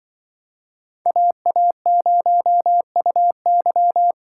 Morse code, heard as a keyed tone, about 23 wpm, 700 Hz. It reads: AA0UY
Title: Speed Racing - USA 2x2 Call Signs 20wpm